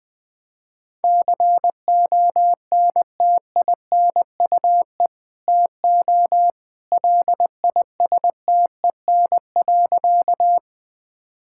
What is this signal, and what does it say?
Morse code, heard as a keyed tone, about 20 wpm, 700 Hz. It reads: CONTINUE TO LISTEN.
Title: Sentences from Top 1000 Words - Review Part 6 - 20wpm